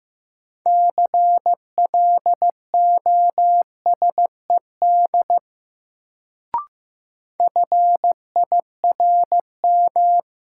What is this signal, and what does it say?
Morse code, heard as a keyed tone, about 15 wpm, 700 Hz. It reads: CLOSED FIRM
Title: Sentences from Top 1400 Words - New Word Drill Alt - 15wpm